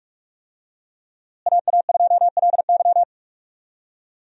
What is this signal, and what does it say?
Morse code, heard as a keyed tone, about 46 wpm, 700 Hz. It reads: AA2LY
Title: USA Call Signs - 5 Characters - Repeated 3x - 45wpm